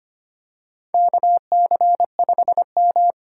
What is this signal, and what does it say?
Morse code, heard as a keyed tone, about 25 wpm, 700 Hz. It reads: KC5M
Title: USA Call Signs - 4x Character Spacing - 25wpm